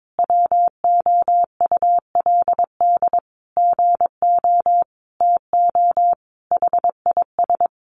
Morse code, heard as a keyed tone, about 22 wpm, 700 Hz. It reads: WOULD GO TO HIS
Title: Sentences from Top 600 Words 22wpm